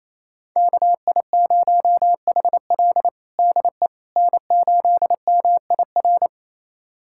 Morse code, heard as a keyed tone, about 28 wpm, 700 Hz. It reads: KI0HL DE N8MIR
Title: USA Callsign DE Callsign - 5 Characters - 28wpm